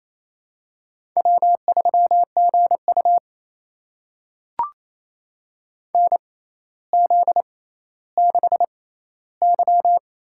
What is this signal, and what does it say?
Morse code, heard as a keyed tone, about 28 wpm, spaced as for 14 wpm, 700 Hz. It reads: W3GU NZ6Y
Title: USA Call Signs - 4 Characters - 6x Character Spacing - 28wpm